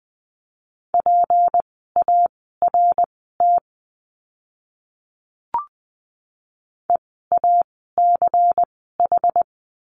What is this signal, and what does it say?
Morse code, heard as a keyed tone, about 20 wpm, spaced as for 15 wpm, 700 Hz. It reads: PART EACH
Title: 192 Lesson 31f - Letter v - Words 20wpm